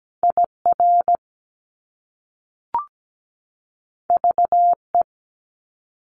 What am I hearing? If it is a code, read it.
Morse code, heard as a keyed tone, about 17 wpm, 700 Hz. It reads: IR VE
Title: Top Two Letter Combinations in Words 17wpm